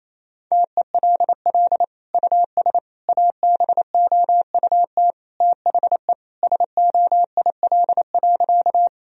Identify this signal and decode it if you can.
Morse code, heard as a keyed tone, about 28 wpm, 700 Hz. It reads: TELL US ABOUT THE SOIL.
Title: Sentences from Top 1000 Words - Review Part 4 - 28wpm